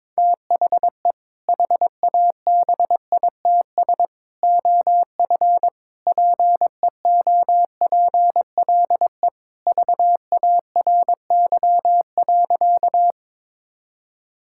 Morse code, heard as a keyed tone, about 22 wpm, 700 Hz. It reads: THE HABITS OF PEOPLE VARY.
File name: practice / 22wpm / Sentences with 2-Word Phrases - 22wpm